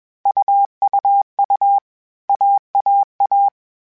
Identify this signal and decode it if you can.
Morse code, heard as a keyed tone, about 21 wpm, 800 Hz. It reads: UUU AAA
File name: Groups@20 / Groups@20 mp3s / AWUNDG@20